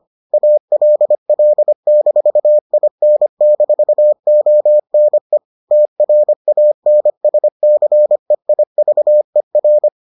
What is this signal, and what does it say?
Morse code, heard as a keyed tone, about 25 wpm, 600 Hz. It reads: ALL-IN-ONE TRANSCEIVER